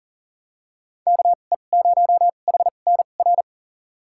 Morse code, heard as a keyed tone, about 40 wpm, spaced as for 29 wpm, 700 Hz. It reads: KE0HNR
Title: USA Call Signs - 6 Characters - 2x Character Spacing - 40wpm